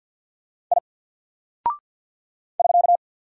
Morse code, heard as a keyed tone, about 48 wpm, 700 Hz. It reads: I <SK>
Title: Single Character (Letter:Number:Punctuation:Prosign) Alt 50wpm